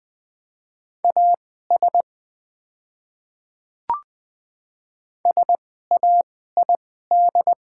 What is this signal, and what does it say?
Morse code, heard as a keyed tone, about 20 wpm, spaced as for 15 wpm, 700 Hz. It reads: AS SAID
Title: 2x Character Spacing - Top 100 Words - 20wpm